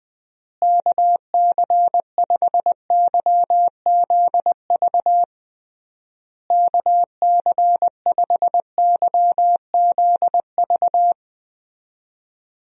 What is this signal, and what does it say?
Morse code, heard as a keyed tone, about 20 wpm, 700 Hz. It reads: KC5YZV KC5YZV
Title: USA Call Signs - 6 Characters - Repeated 3x - 20wpm